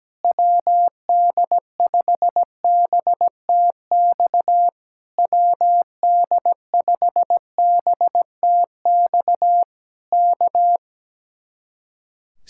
Morse code, heard as a keyed tone, about 17 wpm, 700 Hz. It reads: WD5BTX WD5BTX K
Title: Speed-Racing Common QSO Phrases Encore - 17wpm